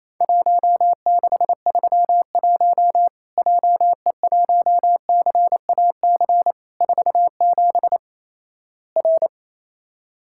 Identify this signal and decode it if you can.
Morse code, heard as a keyed tone, about 28 wpm, 700 Hz. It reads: 1631 JE1CAC 47 R
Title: Worked All Europe DX Contest - QTC Exchange - 28wpm